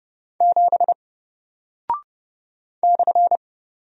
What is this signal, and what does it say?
Morse code, heard as a keyed tone, about 30 wpm, 700 Hz. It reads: 7 /
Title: Single Character (Letter:Number:Punctuation:Prosign) Alt 30wpm